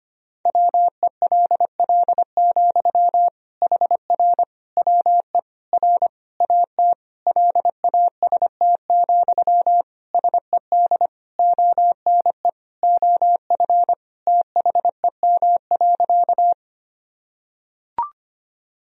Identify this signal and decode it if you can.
Morse code, heard as a keyed tone, about 25 wpm, 700 Hz. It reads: WELL, HR WE R AT LAST, SED ONE OF THEM.
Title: Sentences from Top 300 Words and Abbreviations 25wpm